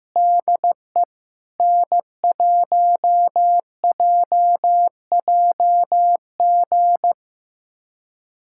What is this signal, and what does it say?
Morse code, heard as a keyed tone, about 15 wpm, 700 Hz. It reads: DE N1JJG
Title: USA Callsign DE Callsign 15wpm